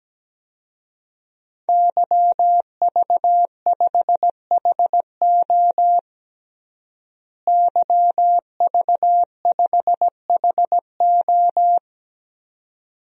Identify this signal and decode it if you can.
Morse code, heard as a keyed tone, about 17 wpm, 700 Hz. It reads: YV5HO YV5HO
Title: DXCC-100 Call Signs - 5 Characters - Repeated 2x - 17wpm